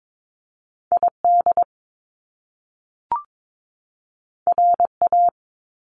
Morse code, heard as a keyed tone, about 22 wpm, 700 Hz. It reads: ID RA
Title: Top Two Letter Combinations in Words 22wpm